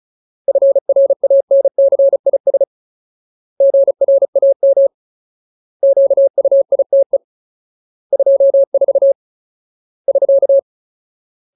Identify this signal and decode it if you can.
Morse code, heard as a keyed tone, about 34 wpm, 550 Hz. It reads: FRANCIS GRAM QUITE 24 <SK>
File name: Ultimate Word List / 35WPM Ultimate